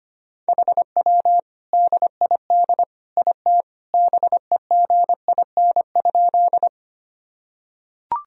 Morse code, heard as a keyed tone, about 25 wpm, 700 Hz. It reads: HW DID IT BEGIN?
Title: Sentences from Top 300 Words and Abbreviations 25wpm